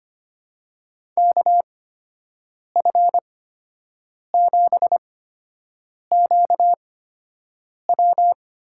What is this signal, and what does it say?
Morse code, heard as a keyed tone, about 25 wpm, spaced as for 7 wpm, 700 Hz. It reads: KF7QW
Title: USA Call Signs - 8x Character Spacing - 25wpm